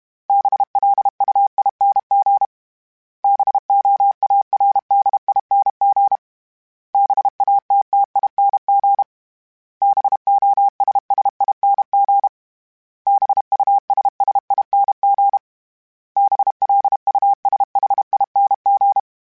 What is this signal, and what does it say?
Morse code, heard as a keyed tone, about 32 wpm, 800 Hz. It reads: BLUING BOARDING BATTING BOSSING BUSSING BLUSHING
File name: Special INGs-mp3s / Special INGs-mp3s / Sngl Ing @30